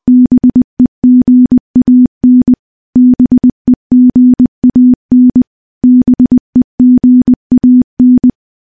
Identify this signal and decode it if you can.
Morse code, heard as a keyed tone, about 20 wpm, 250 Hz. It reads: BEGAN BEGAN BEGAN
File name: morse-training-20WPM